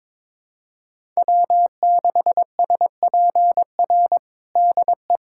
Morse code, heard as a keyed tone, about 22 wpm, 700 Hz. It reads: W6SPR DE
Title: USA Callsign DE Callsign - 5 Characters - 22wpm